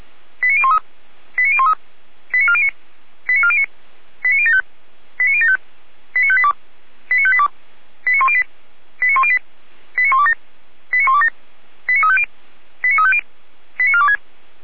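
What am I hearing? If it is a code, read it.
Five-tone selective call (ZVEI), heard as quick runs of five tones, tones of about 70 ms. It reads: 89013 89013 89409 89409 89075 89075 89582 89582 89108 89108 89127 89127 89350 89350 89426
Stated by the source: ZVEI five-tone selective call 89370, SDR recording.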